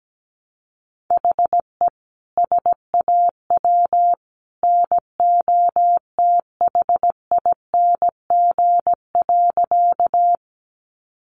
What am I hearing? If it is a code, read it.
Morse code, heard as a keyed tone, about 17 wpm, 700 Hz. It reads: HE SAW NOTHING.
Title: Sentences from Top 500 Words 17wpm